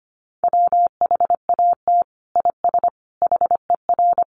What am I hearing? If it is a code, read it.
Morse code, heard as a keyed tone, about 25 wpm, 700 Hz. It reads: WHAT IS HER
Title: Sentences from Top 1500 Words 25wpm